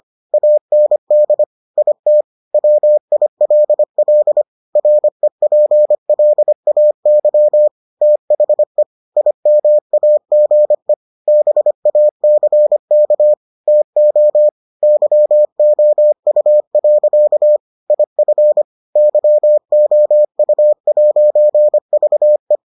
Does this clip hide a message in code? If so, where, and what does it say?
Yes: Morse code from 0.3 to 22.6 s, heard as a keyed tone, about 25 wpm, 600 Hz. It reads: AND IT WILL REPLAY THE IMAGE BACK TO YOU. IF YOU'VE